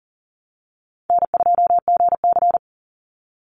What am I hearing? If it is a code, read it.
Morse code, heard as a keyed tone, about 38 wpm, 700 Hz. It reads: N2GC
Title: Easy Call Signs - 40wpm